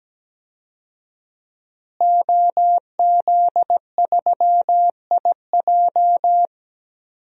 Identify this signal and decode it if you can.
Morse code, heard as a keyed tone, about 17 wpm, 700 Hz. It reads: OZ3IJ
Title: DXCC-100 Call Signs - 5 Characters - Repeated 3x - 17wpm